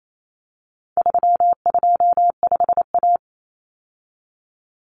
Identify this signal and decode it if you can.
Morse code, heard as a keyed tone, about 28 wpm, 700 Hz. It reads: 325A
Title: Common Cut Numbers - 4-Digit 28wpm